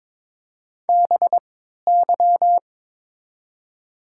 Morse code, heard as a keyed tone, about 22 wpm, spaced as for 13 wpm, 700 Hz. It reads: BY
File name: Top 90 QSO Elements - 3x Character Spacing - 22wpm